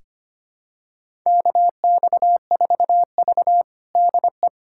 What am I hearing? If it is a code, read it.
Morse code, heard as a keyed tone, about 25 wpm, 700 Hz. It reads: KX4V DE
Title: USA Callsign DE Callsign 25wpm